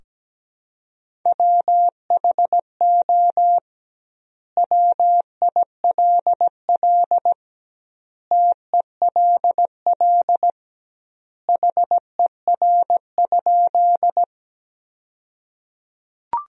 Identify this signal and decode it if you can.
Morse code, heard as a keyed tone, about 17 wpm, 700 Hz. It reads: WHO WILL TELL HER?
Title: 2x Word-Spacing - Sentences from Top 200 Words - 17wpm